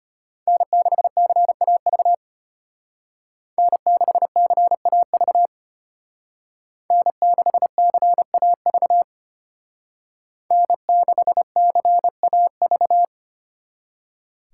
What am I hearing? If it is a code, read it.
Morse code, heard as a keyed tone, about 31 wpm, 700 Hz. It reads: N6CAV N6CAV N6CAV N6CAV